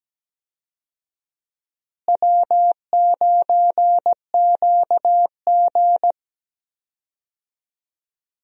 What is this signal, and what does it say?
Morse code, heard as a keyed tone, about 17 wpm, 700 Hz. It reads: W9QG
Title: USA Call Signs - 4 Characters - Repeated 2x - 17wpm